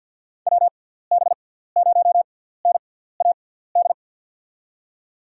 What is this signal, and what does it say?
Morse code, heard as a keyed tone, about 48 wpm, spaced as for 17 wpm, 700 Hz. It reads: WB0NAD
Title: USA Call Signs - 6 Characters - 6x Character Spacing - 50wpm